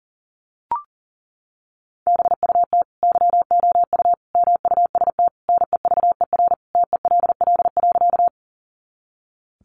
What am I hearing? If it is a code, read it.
Morse code, heard as a keyed tone, about 40 wpm, 700 Hz. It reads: BUT YOU MUST NEVER TELL.
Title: Sentences from Top 300 Words 40wpm